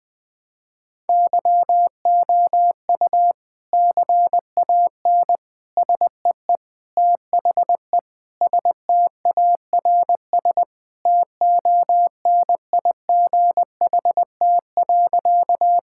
Morse code, heard as a keyed tone, about 20 wpm, 700 Hz. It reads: YOU CAN SEE THE STARS TONIGHT.
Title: Sentences with 3-Word Phrases - 20wpm